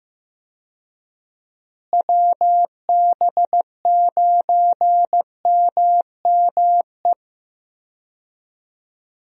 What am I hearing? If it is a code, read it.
Morse code, heard as a keyed tone, about 15 wpm, 700 Hz. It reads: WB9MME
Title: USA Call Signs - 6 Characters - Repeated 3x - 15wpm